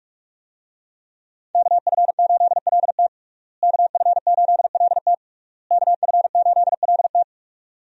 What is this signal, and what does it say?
Morse code, heard as a keyed tone, about 46 wpm, 700 Hz. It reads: KF8LT KF8LT KF8LT
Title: USA Call Signs - 5 Characters - Repeated 3x - 45wpm